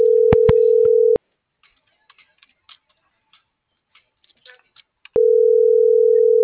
Ringback tone (North America), 440 + 480 Hz.